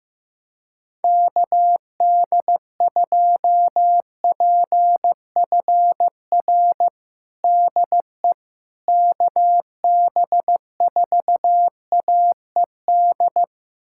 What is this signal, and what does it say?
Morse code, heard as a keyed tone, about 15 wpm, 700 Hz. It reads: KD2PFR DE KB4AED
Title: USA Callsign DE Callsign - 6 Characters - 15wpm